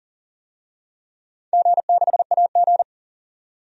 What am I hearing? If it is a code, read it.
Morse code, heard as a keyed tone, about 40 wpm, 700 Hz. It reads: G6AG
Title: DXCC-100 Call Signs - 4 Characters - 40wpm